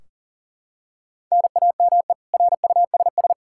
Morse code, heard as a keyed tone, about 40 wpm, 700 Hz. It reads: NAME RUSS
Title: Practice QSO - Exchange 1 - Beginning - 40wpm